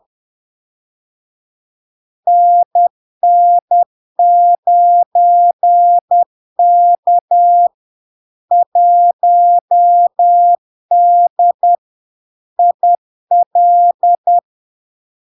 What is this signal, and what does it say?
Morse code, heard as a keyed tone, about 10 wpm, 700 Hz. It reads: NN9K 1D IL
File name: LARK-FD-2022-Log-in-CW-10wpm-700-Hz